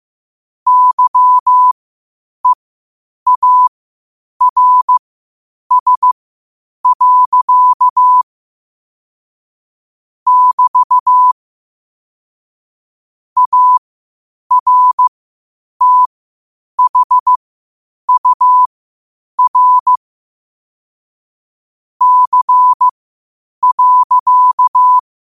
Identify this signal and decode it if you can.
Morse code, heard as a keyed tone, about 15 wpm, spaced as for 8 wpm, 1000 Hz. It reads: YEARS. <BT> ARTHUR C.
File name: qotdfiles / 8wpm